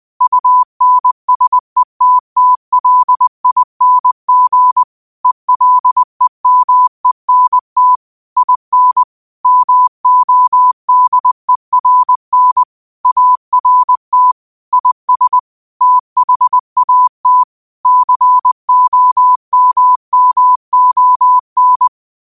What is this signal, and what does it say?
Morse code, heard as a keyed tone, about 20 wpm, 1000 Hz. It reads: UNSETTLING ELEMENT IN MODERN ART IS THAT COMMON
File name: qotdfiles / 20wpm